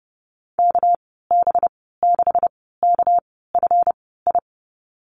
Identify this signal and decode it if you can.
Morse code, heard as a keyed tone, about 30 wpm, spaced as for 17 wpm, 700 Hz. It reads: KB6KFI
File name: Lesson 37g - Letter x - Call Signs 30wpm